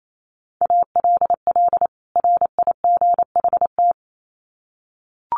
Morse code, heard as a keyed tone, about 28 wpm, 700 Hz. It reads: ALL RIGHT
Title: Sets of 2 Words 28wpm